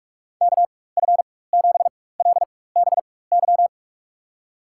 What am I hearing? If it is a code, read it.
Morse code, heard as a keyed tone, about 46 wpm, spaced as for 21 wpm, 700 Hz. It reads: KF7LBY
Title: USA Call Signs - 6 Characters - 4x Character Spacing - 45wpm